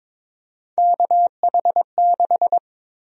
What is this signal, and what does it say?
Morse code, heard as a keyed tone, about 22 wpm, 700 Hz. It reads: KH6
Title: USA Call Sign Prefixes - Repeated 2x - 22wpm